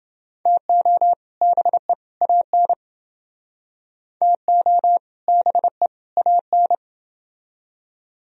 Morse code, heard as a keyed tone, about 29 wpm, 700 Hz. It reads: TO BE AN TO BE AN